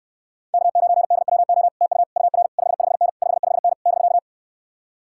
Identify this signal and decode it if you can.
Morse code, heard as a keyed tone, about 68 wpm, 700 Hz. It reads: K9DWO TU UR 55N 55N <BK>